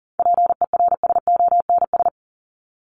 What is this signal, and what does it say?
Morse code, heard as a keyed tone, about 40 wpm, 700 Hz. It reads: PERSONS
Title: ES:S Suffix - Top 100 Words with Base Words 40wpm